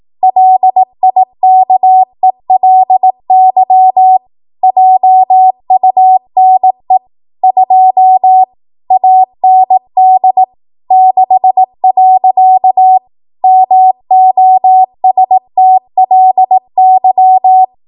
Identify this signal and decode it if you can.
Morse code, heard as a keyed tone, about 18 wpm, 750 Hz. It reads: LIKELY JUNE 2 AND 6. MOSTLY
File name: Morse / 18WPM